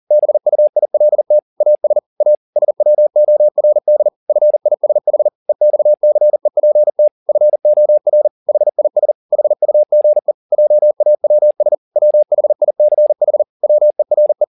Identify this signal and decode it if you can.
Morse code, heard as a keyed tone, about 40 wpm, 600 Hz. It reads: BUILT AS A SWORD FISH EXCEPT FOR HIS HUGE JAWS WHICH WERE
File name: the old man and the sea 2 40wpm